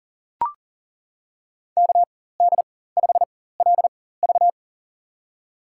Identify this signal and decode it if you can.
Morse code, heard as a keyed tone, about 40 wpm, spaced as for 19 wpm, 700 Hz. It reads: KD5LV
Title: USA Call Signs - 5 Characters - 4x Character Spacing - 40wpm